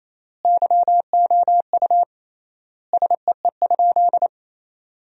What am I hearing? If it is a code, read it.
Morse code, heard as a keyed tone, about 28 wpm, 700 Hz. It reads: YOU SEE?
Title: 3x Word-Spacing - Sentences from Top 100 Words - 28wpm